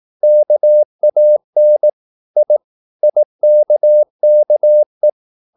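Morse code, heard as a keyed tone, about 18 wpm, 600 Hz. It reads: KAN I IKKE
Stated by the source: Slow (QRS) Morse practice at 18 wpm, CW med Gnister.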